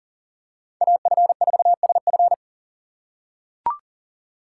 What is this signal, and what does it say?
Morse code, heard as a keyed tone, about 40 wpm, 700 Hz. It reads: AF4SF
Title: USA Call Signs - 2x Character Spacing - 40wpm